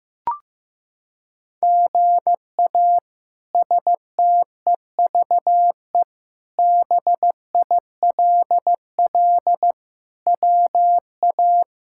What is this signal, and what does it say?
Morse code, heard as a keyed tone, about 15 wpm, 700 Hz. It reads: GA STEVE BILL WA
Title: K1USN SST QSO Practice - 15wpm